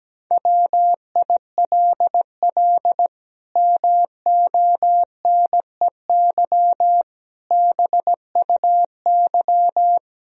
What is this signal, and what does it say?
Morse code, heard as a keyed tone, about 17 wpm, 700 Hz. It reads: WILL MONEY BUY
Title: Sentences from Top 1000 Words - Review Part 4 - 17wpm